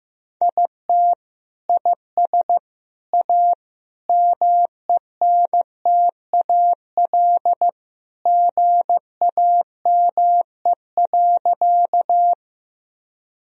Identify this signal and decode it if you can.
Morse code, heard as a keyed tone, about 15 wpm, 700 Hz. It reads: IT IS A MENTAL GAME.
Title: Sentences from Top 1000 Words - Review Part 4 - 15wpm